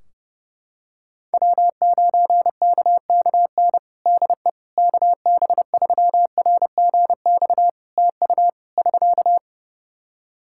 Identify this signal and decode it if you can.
Morse code, heard as a keyed tone, about 30 wpm, 700 Hz. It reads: W9KKN DE KB3RGX TU <SK>
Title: Practice QSO - Exchange 4 - Ending - 30wpm